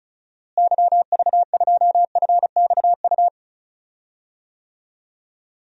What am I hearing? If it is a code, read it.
Morse code, heard as a keyed tone, about 34 wpm, 700 Hz. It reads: YV2FXU
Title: DXCC-100 Call Signs - 6 Characters - Repeated 2x - 35wpm